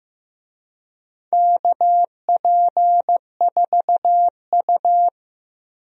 Morse code, heard as a keyed tone, about 15 wpm, 700 Hz. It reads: KP4U
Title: DXCC-100 Call Signs - 4 Characters - 15wpm